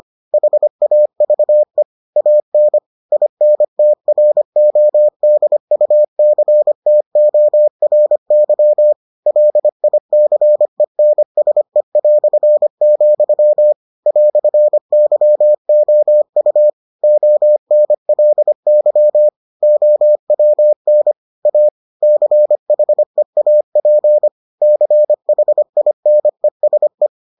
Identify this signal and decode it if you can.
Morse code, heard as a keyed tone, about 25 wpm, 600 Hz. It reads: HAVE AN INTRODUCTORY LICENSE", "YOU ONLY OWN A CHEAP CHINESE